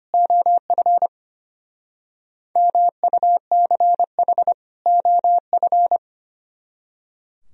Morse code, heard as a keyed tone, about 32 wpm, 700 Hz. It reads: OF MUCH OF